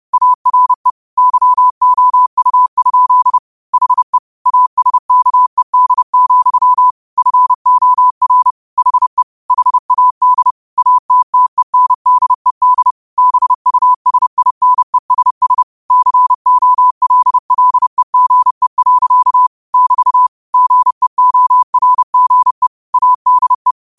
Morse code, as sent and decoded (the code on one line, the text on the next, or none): .- .-. / -.-- --- ..- ..--.. / .... / .- ... -.- . -.. --..-- / ..-. --- .-. / .... / .... .- -.. / .- - - . -. -.. -.. / -... ..- ... .. -. ... ... / -.-. --- .-.. .-.. --. .-.-.- / -...- / --. --- .-. --. / .- -.. .
ARE YOU? HE ASKED, FOR HE HAD ATTENDED BUSINESS COLLEGE. <BT> GEORGE ADE